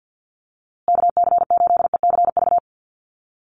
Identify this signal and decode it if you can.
Morse code, heard as a keyed tone, about 50 wpm, 700 Hz. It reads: KC8ECV
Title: USA Call Signs Repeated 2x 50wpm